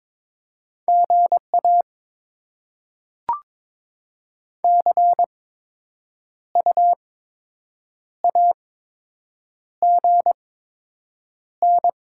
Morse code, heard as a keyed tone, about 22 wpm, spaced as for 7 wpm, 700 Hz. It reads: GA CU AGN
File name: Top 20 QSO Elements - 8x Character Spacing - 22wpm